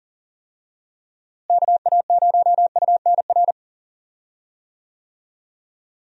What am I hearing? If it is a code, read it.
Morse code, heard as a keyed tone, about 40 wpm, 700 Hz. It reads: KA0UNR
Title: USA Call Signs - 6 Characters - Repeated 2x - 40wpm